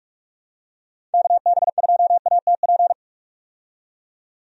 Morse code, heard as a keyed tone, about 46 wpm, 700 Hz. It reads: KB2ATP
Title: USA Call Signs - 6 Characters - 45wpm